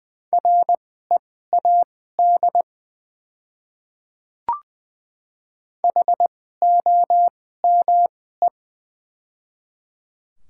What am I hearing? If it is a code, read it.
Morse code, heard as a keyed tone, about 20 wpm, spaced as for 15 wpm, 700 Hz. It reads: READ HOME
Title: 2x Character Spacing - Top 200 Words - 20wpm